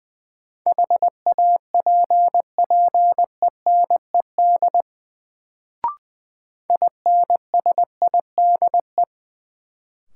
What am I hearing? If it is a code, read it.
Morse code, heard as a keyed tone, about 20 wpm, 700 Hz. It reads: HAPPENED INSIDE
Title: Sentences from Top 1000 Words - Review Part 2 - New Word Drill - 20wpm